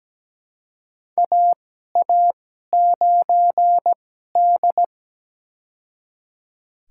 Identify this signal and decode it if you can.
Morse code, heard as a keyed tone, about 17 wpm, spaced as for 12 wpm, 700 Hz. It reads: AA9D
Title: USA Call Signs - 4 Characters - 2x Character Spacing - 17wpm